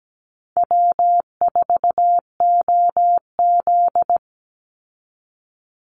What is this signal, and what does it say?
Morse code, heard as a keyed tone, about 17 wpm, 700 Hz. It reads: W4OZ
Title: USA Call Signs 17wpm